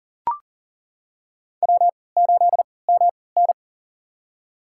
Morse code, heard as a keyed tone, about 40 wpm, spaced as for 23 wpm, 700 Hz. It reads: W8MN
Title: USA Call Signs - 3x Character Spacing - 40wpm